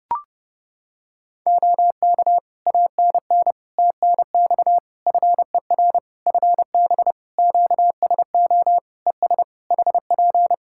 Morse code, heard as a keyed tone, about 30 wpm, 700 Hz. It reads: OK ANN TNX FER FB QSO ES HP
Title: K1USN SST QSO Practice 30wpm